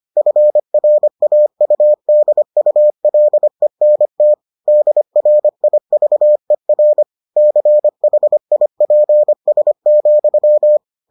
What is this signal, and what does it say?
Morse code, heard as a keyed tone, about 25 wpm, 600 Hz. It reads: FRAUDULENT DRIVER CHIPS,